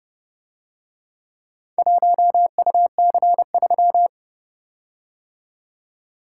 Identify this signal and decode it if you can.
Morse code, heard as a keyed tone, about 31 wpm, 700 Hz. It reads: 1UC3